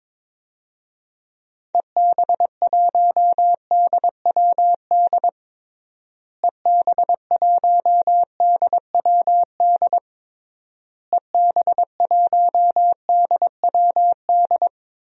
Morse code, heard as a keyed tone, about 22 wpm, 700 Hz. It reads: EB1DWD EB1DWD EB1DWD
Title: DXCC-100 Call Signs - 6 Characters - Repeated 3x - 22wpm